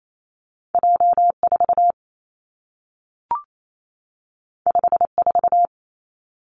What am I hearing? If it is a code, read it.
Morse code, heard as a keyed tone, about 28 wpm, 700 Hz. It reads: J 4 5 4
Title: Two Challenging Letter-Number Combinations 28wpm